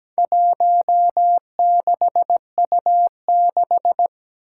Morse code, heard as a keyed tone, about 17 wpm, 700 Hz. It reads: 16U6
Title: Four Challenging Letter-Number Combinations 17wpm